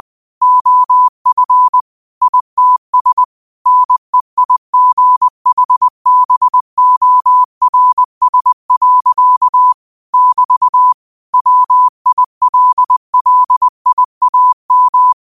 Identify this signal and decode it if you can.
Morse code, heard as a keyed tone, about 20 wpm, 1000 Hz. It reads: OF ITS NEIGHBORS. <BT> WILLIAM
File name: qotdfiles / 20wpm